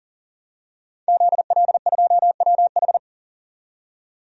Morse code, heard as a keyed tone, about 40 wpm, 700 Hz. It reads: ZL2WH
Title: DXCC-100 Call Signs - 5 Characters - 40wpm